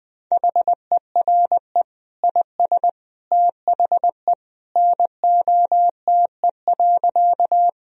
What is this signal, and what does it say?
Morse code, heard as a keyed tone, about 21 wpm, 700 Hz. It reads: HERE IS THE NOTE.
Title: Sentences from Top 1000 Words - Review Part 2 - 20wpm